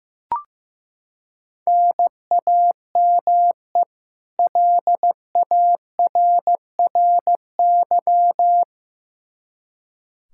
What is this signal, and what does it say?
Morse code, heard as a keyed tone, about 15 wpm, 700 Hz. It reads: NAME LARRY
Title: Practice QSO - Exchange 1 - Beginning - 15wpm